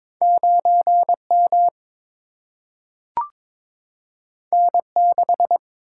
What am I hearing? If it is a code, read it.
Morse code, heard as a keyed tone, about 22 wpm, 700 Hz. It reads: 9 M N 6
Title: Single-Letter Single-Number 22wpm